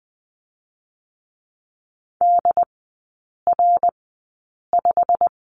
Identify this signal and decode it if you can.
Morse code, heard as a keyed tone, about 20 wpm, spaced as for 8 wpm, 690 Hz. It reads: DR5
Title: 087 Lesson 15e - Number 5 - Three Characters 20wpm